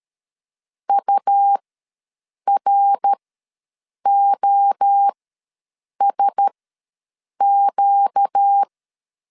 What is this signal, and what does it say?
Morse code, heard as a keyed tone, about 13 wpm, spaced as for 7 wpm, 800 Hz. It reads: UROSQ
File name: LEZIONI / 8 YJ45 / 8 YJ ws 12wpm 3342